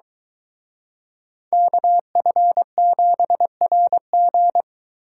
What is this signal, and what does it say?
Morse code, heard as a keyed tone, about 23 wpm, 700 Hz. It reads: KF7RG